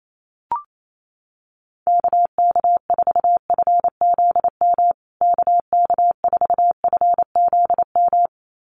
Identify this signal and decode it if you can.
Morse code, heard as a keyed tone, about 28 wpm, 700 Hz. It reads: KK4FZM KK4FZM
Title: USA Call Signs Repeated 2x 28wpm